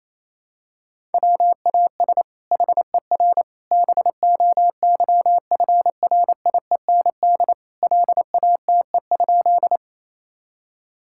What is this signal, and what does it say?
Morse code, heard as a keyed tone, about 28 wpm, 700 Hz. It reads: WAS HER BOYFRIEND LATE?
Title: Sentences from Top 1000 Words - Review Part 5 - 28wpm